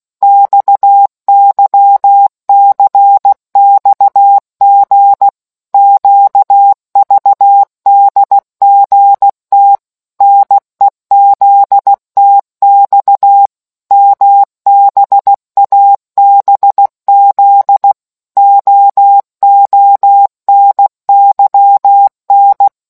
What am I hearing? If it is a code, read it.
Morse code, heard as a keyed tone, about 16 wpm, 800 Hz. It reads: XYCXG QVDGT NEZTX MBABZ OONYN